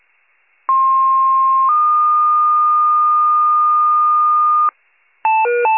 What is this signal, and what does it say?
Two-tone page, heard as a one-second tone, then a three-second tone, about 1060 then 1220 Hz, with noise. It is heard from 0.7 to 4.7 s.